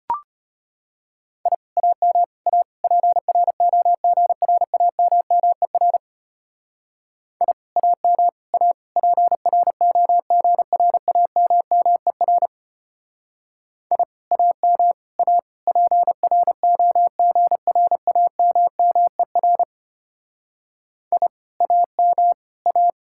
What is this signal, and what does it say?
Morse code, heard as a keyed tone, about 33 wpm, 700 Hz. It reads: I AM A PROGRAMMER I AM A PROGRAMMER I AM A PROGRAMMER I AM A